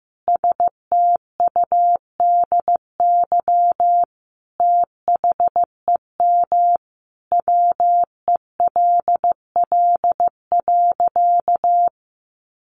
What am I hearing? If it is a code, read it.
Morse code, heard as a keyed tone, about 15 wpm, 700 Hz. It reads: STUDY THEM WELL.
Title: Sentences from Top 300 Words 15wpm